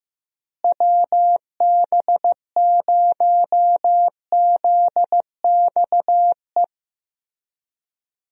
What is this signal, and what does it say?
Morse code, heard as a keyed tone, about 15 wpm, 700 Hz. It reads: WB0ZXE
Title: USA Call Signs - 6 Characters - Repeated 3x - 15wpm